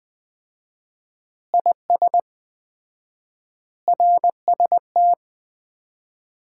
Morse code, heard as a keyed tone, about 20 wpm, 700 Hz. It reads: IS RST
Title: Common QSO Phrases - 4x Word-Spacing - 20wpm